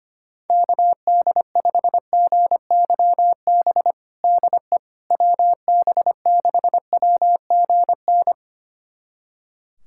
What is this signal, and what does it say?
Morse code, heard as a keyed tone, about 25 wpm, 700 Hz. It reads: KD5GYB DE WB6WGN
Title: USA Callsign DE Callsign - 6 Characters - 25wpm